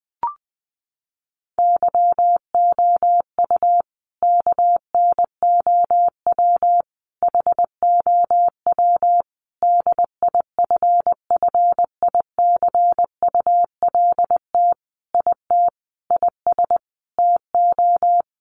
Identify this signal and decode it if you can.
Morse code, heard as a keyed tone, about 20 wpm, 700 Hz. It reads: YOU KNOW HOW DIFFICULT IT IS TO
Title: Sentences from Top 1100 Words 20wpm